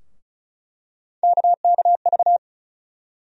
Morse code, heard as a keyed tone, about 34 wpm, 700 Hz. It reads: KKV
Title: USA Call Sign Postfixes - 3 Letters 35wpm